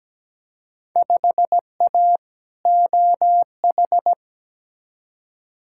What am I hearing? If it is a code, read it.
Morse code, heard as a keyed tone, about 17 wpm, 700 Hz. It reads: 5A OH
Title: Field Day Exchanges Encore - 17wpm